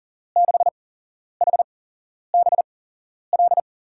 Morse code, heard as a keyed tone, about 41 wpm, spaced as for 11 wpm, 700 Hz. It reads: 6HBL
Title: USA Call Signs - 6 Characters - 8x Character Spacing - 40wpm